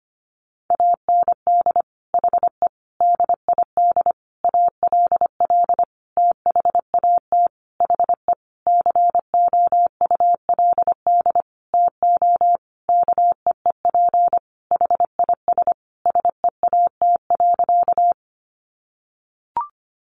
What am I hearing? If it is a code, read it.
Morse code, heard as a keyed tone, about 25 wpm, 700 Hz. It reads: AND HE DID ALL THAT HE COULD TO KEEP HIS SEAT.
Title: Sentences from Top 1500 Words 25wpm